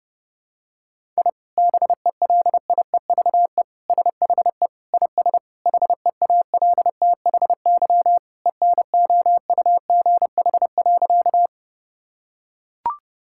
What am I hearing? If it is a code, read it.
Morse code, heard as a keyed tone, about 30 wpm, 700 Hz. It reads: I BELIEVE SHE IS HEALTHY ENOUGH.
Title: Sentences from Top 1000 Words - Review Part 5 - 30wpm